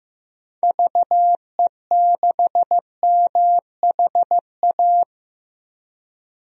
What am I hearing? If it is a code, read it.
Morse code, heard as a keyed tone, about 15 wpm, 700 Hz. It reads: VE6MHA
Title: DXCC-100 Call Signs - 6 Characters - 15wpm